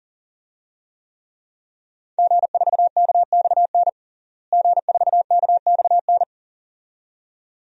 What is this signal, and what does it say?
Morse code, heard as a keyed tone, about 40 wpm, 700 Hz. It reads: G4KXN G4KXN
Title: DXCC-100 Call Signs - 5 Characters - Repeated 2x - 40wpm